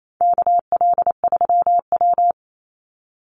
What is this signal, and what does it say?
Morse code, heard as a keyed tone, about 28 wpm, 700 Hz. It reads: KL3W
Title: Call Signs Repeated 2x - 28wpm